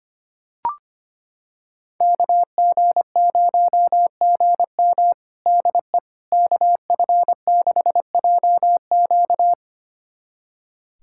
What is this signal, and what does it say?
Morse code, heard as a keyed tone, about 25 wpm, 700 Hz. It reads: KG0GM DE KF6JQ
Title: USA Callsign DE Callsign - 5 Characters - 25wpm